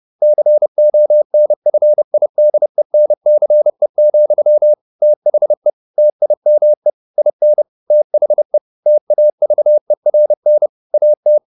Morse code, heard as a keyed tone, about 30 wpm, 600 Hz. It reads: CONFIDENCE, THE TIME IN THE TAVERN AT
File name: the old man and the sea / the old man and the sea 1 30wpm